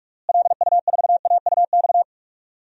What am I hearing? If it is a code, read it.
Morse code, heard as a keyed tone, about 46 wpm, 700 Hz. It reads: LU4AUX
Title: DXCC-100 Call Signs - 6 Characters - Repeated 2x - 45wpm